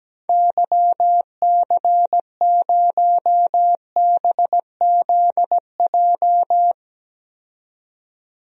Morse code, heard as a keyed tone, about 17 wpm, 700 Hz. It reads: YC0BZJ
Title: DXCC-100 Call Signs - 6 Characters - 17wpm